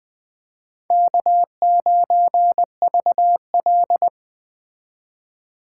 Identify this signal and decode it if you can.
Morse code, heard as a keyed tone, about 21 wpm, 700 Hz. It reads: K9VL